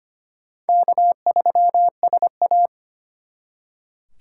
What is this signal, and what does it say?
Morse code, heard as a keyed tone, about 34 wpm, spaced as for 30 wpm, 700 Hz. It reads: K3SA
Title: Speed Racing - USA 1x2 Call Signs 25wpm